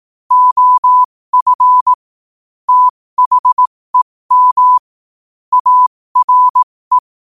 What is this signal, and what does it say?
Morse code, heard as a keyed tone, about 18 wpm, spaced as for 15 wpm, 1000 Hz. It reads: OF THEM ARE
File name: qotdfiles / 15wpm